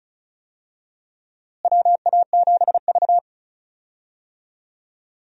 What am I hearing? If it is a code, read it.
Morse code, heard as a keyed tone, about 36 wpm, 700 Hz. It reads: WA7V
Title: USA Call Signs - 4 Characters - Repeated 2x - 35wpm